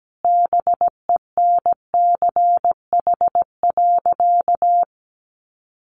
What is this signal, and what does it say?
Morse code, heard as a keyed tone, about 17 wpm, 700 Hz. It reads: BENCH.
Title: Sentences from Top 1400 Words 17wpm